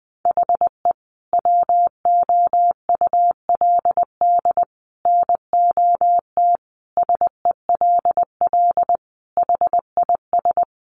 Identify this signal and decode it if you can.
Morse code, heard as a keyed tone, about 20 wpm, 700 Hz. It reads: HE WOULD NOT SELL HIS
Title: Sentences from Top 1100 Words 20wpm